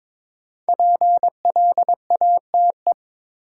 Morse code, heard as a keyed tone, about 22 wpm, 700 Hz. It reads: PLATE
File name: 5 Letter Words - First 3 Letters Given - 22wpm